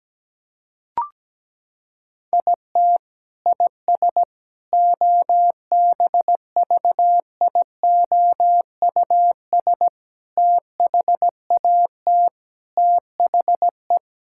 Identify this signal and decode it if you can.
Morse code, heard as a keyed tone, about 17 wpm, 700 Hz. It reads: IT IS OBVIOUS THAT THE
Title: Sentences from Top 1600 Words 17wpm